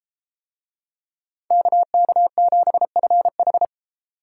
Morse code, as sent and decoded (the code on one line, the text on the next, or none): -.- -.- --... ..-. ....
KK7FH